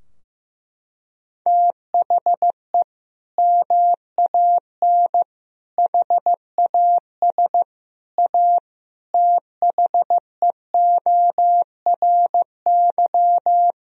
Morse code, heard as a keyed tone, about 16 wpm, 700 Hz. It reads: THE MAN HAS A THEORY
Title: Sentences from Top 1000 Words - Review Part 3 - 15wpm